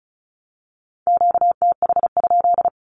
Morse code, heard as a keyed tone, about 34 wpm, 700 Hz. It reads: QTH?
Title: Q-Codes Repeated 3x 35wpm